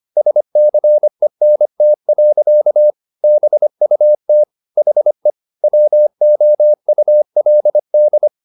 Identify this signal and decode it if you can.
Morse code, heard as a keyed tone, about 25 wpm, 600 Hz. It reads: SCENT. BUT HE WOULD